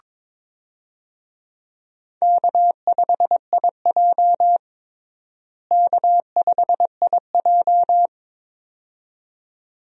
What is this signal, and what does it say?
Morse code, heard as a keyed tone, about 22 wpm, 700 Hz. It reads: K5IJ K5IJ
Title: USA Call Signs - 4 Characters - Repeated 2x - 22wpm